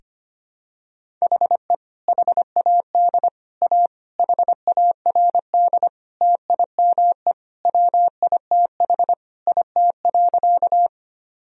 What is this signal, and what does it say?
Morse code, heard as a keyed tone, about 25 wpm, 700 Hz. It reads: HE HAD A HARD TIME WITH IT.
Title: Sentences from Top 1000 Words - Review Part 1 - 25wpm